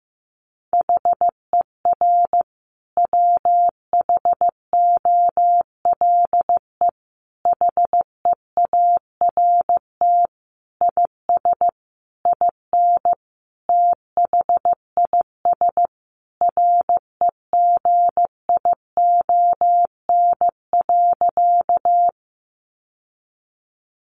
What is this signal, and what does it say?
Morse code, heard as a keyed tone, about 15 wpm, 700 Hz. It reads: HER WHOLE HEART IS IN THIS REGION.
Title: Sentences from Top 1500 Words 15wpm